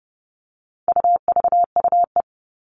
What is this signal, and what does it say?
Morse code, heard as a keyed tone, about 30 wpm, 700 Hz. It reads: UVUE
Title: All Cut Numbers - 4-Digit 30wpm